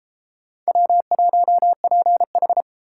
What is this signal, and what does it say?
Morse code, heard as a keyed tone, about 32 wpm, 700 Hz. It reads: W1PH